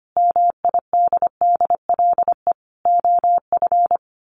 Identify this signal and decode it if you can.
Morse code, heard as a keyed tone, about 25 wpm, 700 Hz. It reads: MIDDLE OF
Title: Sets of 5 Words Encore 25wpm